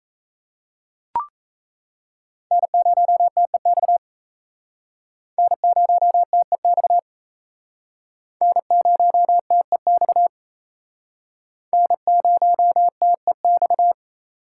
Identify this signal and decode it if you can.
Morse code, heard as a keyed tone, about 37 wpm, 700 Hz. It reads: N0TEX N0TEX N0TEX N0TEX